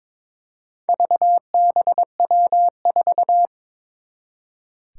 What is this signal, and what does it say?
Morse code, heard as a keyed tone, about 22 wpm, 700 Hz. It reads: VBW4
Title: Four Challenging Letter-Number Combinations 22wpm